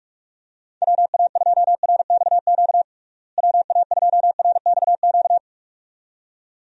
Morse code, heard as a keyed tone, about 46 wpm, 700 Hz. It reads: WA2RXQ WA2RXQ
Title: USA Call Signs - 6 Characters - Repeated 2x - 45wpm